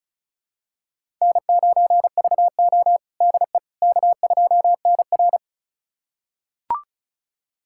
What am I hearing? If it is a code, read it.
Morse code, heard as a keyed tone, about 36 wpm, 700 Hz. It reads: N9VO DE K2NR
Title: USA Callsign DE Callsign - 4 Characters - 35wpm